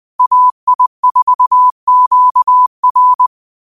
Morse code, heard as a keyed tone, about 20 wpm, 1000 Hz. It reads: AI4QR
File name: qotdfiles / 20wpm